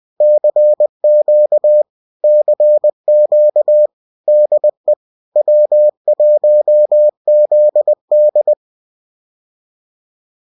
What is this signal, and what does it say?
Morse code, heard as a keyed tone, about 20 wpm, 600 Hz. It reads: CQ CQ DE W1ZD
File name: CQ-de-W1ZD